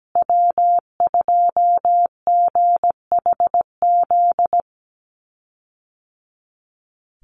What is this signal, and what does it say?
Morse code, heard as a keyed tone, about 17 wpm, 700 Hz. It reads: W2GHZ
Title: USA Call Signs Repeated 2x 17wpm